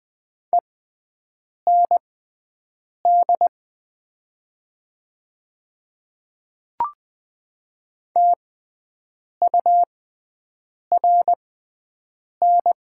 Morse code, heard as a keyed tone, about 20 wpm, spaced as for 7 wpm, 700 Hz. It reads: END TURN
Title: 6x Character Spacing - Top 300 Words - 20wpm